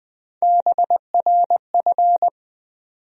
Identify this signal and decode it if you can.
Morse code, heard as a keyed tone, about 20 wpm, 700 Hz. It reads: BRF
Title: USA Call Sign Postfixes - 3 Letters 20wpm